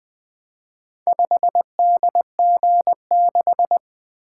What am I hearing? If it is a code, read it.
Morse code, heard as a keyed tone, about 21 wpm, 700 Hz. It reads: 5DG6
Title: Four Letter-Number Combinations - 20wpm